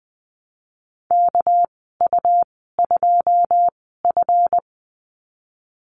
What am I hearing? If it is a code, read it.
Morse code, heard as a keyed tone, about 20 wpm, spaced as for 15 wpm, 700 Hz. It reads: KU2F
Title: 238 Lesson 38g - Letter q - Call Signs 20wpm